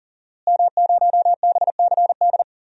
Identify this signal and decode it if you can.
Morse code, heard as a keyed tone, about 40 wpm, 700 Hz. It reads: M0BCD
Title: DXCC-100 Call Signs - 5 Characters - Repeated 3x - 40wpm